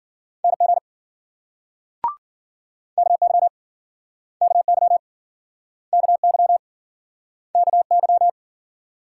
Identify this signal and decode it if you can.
Morse code, heard as a keyed tone, about 55 wpm, 700 Hz. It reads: NG KY KY KY KY